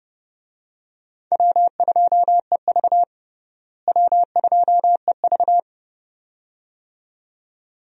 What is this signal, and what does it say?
Morse code, heard as a keyed tone, about 30 wpm, 700 Hz. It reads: W2EV W2EV
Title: USA Call Signs - 4 Characters - Repeated 2x - 30wpm